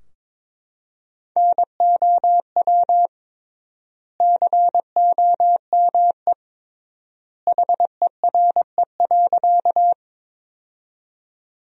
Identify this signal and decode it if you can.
Morse code, heard as a keyed tone, about 22 wpm, 700 Hz. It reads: NOW COME HERE.
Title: 3x Word-Spacing - Sentences from Top 200 Words - 22wpm